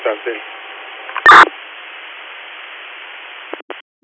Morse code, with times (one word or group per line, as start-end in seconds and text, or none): none